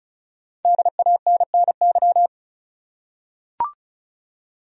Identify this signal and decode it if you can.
Morse code, heard as a keyed tone, about 36 wpm, 700 Hz. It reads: DANNY
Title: USA Common Names - Max 5 Letters - 35wpm